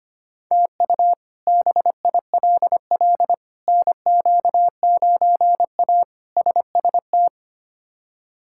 Morse code, heard as a keyed tone, about 25 wpm, 700 Hz. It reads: TU BILL NQ9A SST
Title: K1USN SST QSO Practice - 25wpm